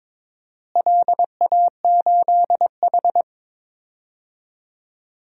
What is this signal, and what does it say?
Morse code, heard as a keyed tone, about 22 wpm, 700 Hz. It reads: LA8H
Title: DXCC-100 Call Signs - 4 Characters - Repeated 3x - 22wpm